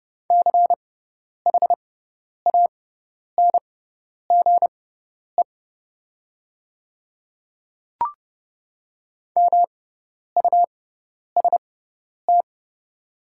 Morse code, heard as a keyed tone, about 30 wpm, spaced as for 10 wpm, 700 Hz. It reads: CHANGE MUST
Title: 6x Character Spacing - Top 200 Words - 30wpm